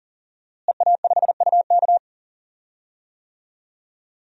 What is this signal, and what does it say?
Morse code, heard as a keyed tone, about 40 wpm, 700 Hz. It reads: EA5UK
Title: DXCC-100 Call Signs - 5 Characters - Repeated 3x - 40wpm